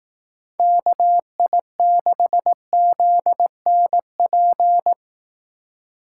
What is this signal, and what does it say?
Morse code, heard as a keyed tone, about 18 wpm, 700 Hz. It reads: KI6ZNP
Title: Speed Racing - USA 2x3 Call Signs 15wpm